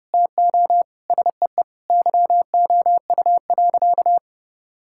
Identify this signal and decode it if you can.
Morse code, heard as a keyed tone, about 30 wpm, 700 Hz. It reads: TO SEE YOU.
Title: Sentences from Top 1000 Words - Review Part 5 - 30wpm